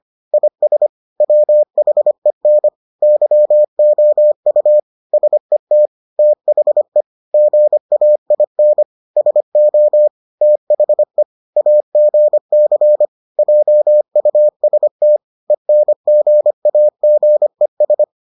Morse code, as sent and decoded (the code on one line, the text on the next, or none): .. ... / .-- .... -. / -.-- --- ..- / ... - / - .... / --. .- .. -. / ... --- / - .... / .- --. -.-. / .--- ..- ... - / . -. --. .- --. ...
IS WHEN YOU SET THE GAIN SO THE AGC JUST ENGAGES